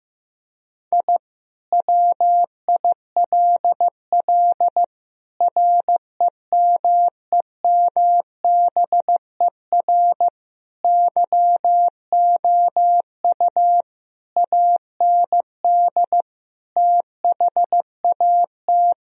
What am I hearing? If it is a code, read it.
Morse code, heard as a keyed tone, about 15 wpm, 700 Hz. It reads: I WILL REMEMBER YOU AND THAT
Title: Sentences from Top 1000 Words - Review Part 2 - 15wpm